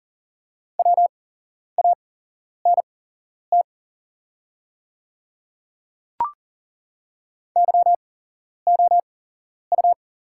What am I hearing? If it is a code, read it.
Morse code, heard as a keyed tone, about 40 wpm, spaced as for 11 wpm, 700 Hz. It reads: WANT YOU
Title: 8x Character Spacing - Top 200 Words - 40wpm